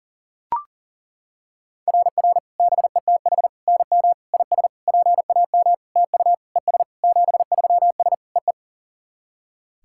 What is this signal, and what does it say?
Morse code, heard as a keyed tone, about 40 wpm, 690 Hz. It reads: RR BETH NM IS PAM TU ES 73S EE
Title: SOTA QSO Practice 40wpm